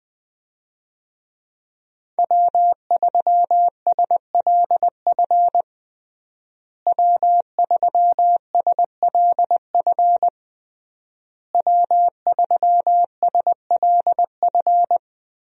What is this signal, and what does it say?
Morse code, heard as a keyed tone, about 20 wpm, 700 Hz. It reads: W3SLF W3SLF W3SLF
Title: USA Call Signs - 5 Characters - Repeated 3x - 20wpm